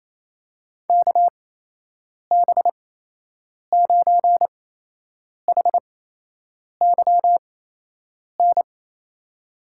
Morse code, heard as a keyed tone, about 28 wpm, spaced as for 8 wpm, 700 Hz. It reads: KB9HYN
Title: USA Call Signs - 6 Characters - 8x Character Spacing - 28wpm